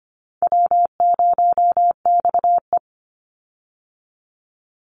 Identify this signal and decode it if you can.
Morse code, heard as a keyed tone, about 25 wpm, 700 Hz. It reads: W0XE
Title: Easy Call Signs - 25wpm